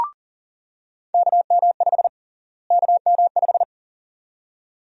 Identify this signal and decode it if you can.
Morse code, heard as a keyed tone, about 40 wpm, 700 Hz. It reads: KM5 KM5
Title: USA Call Sign Prefixes - Repeated 2x - 40wpm